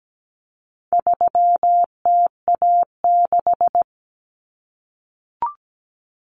Morse code, heard as a keyed tone, about 17 wpm, 700 Hz. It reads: 3TA6
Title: Common Cut Numbers - 4-Digit 17wpm